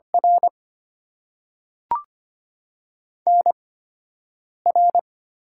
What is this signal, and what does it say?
Morse code, heard as a keyed tone, about 25 wpm, spaced as for 7 wpm, 700 Hz. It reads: R NR
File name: Top 90 QSO Elements - 8x Character Spacing - 25wpm